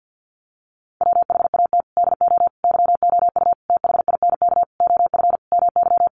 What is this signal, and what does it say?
Morse code, heard as a keyed tone, about 50 wpm, 700 Hz. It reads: WHAT DO YOU THINK OF MY
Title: Sentences from Top 1200 Words 50wpm